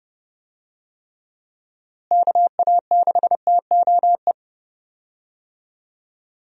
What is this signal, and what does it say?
Morse code, heard as a keyed tone, about 30 wpm, 700 Hz. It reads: KA6TOE
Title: USA Call Signs - 6 Characters - Repeated 2x - 30wpm